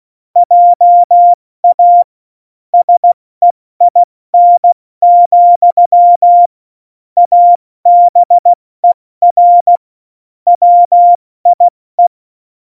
Morse code, heard as a keyed tone, about 16 wpm, 700 Hz. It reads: JA SEIN, ABER WIE
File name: Morsekurs / Witze / Witz 19. Dezember 2017, 70 F80